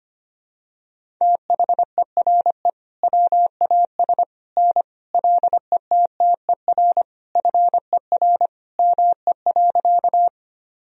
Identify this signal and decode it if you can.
Morse code, heard as a keyed tone, about 25 wpm, 700 Hz. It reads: THERE WAS N LETTER FER ME.
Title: Sentences from Top 300 Words and Abbreviations 25wpm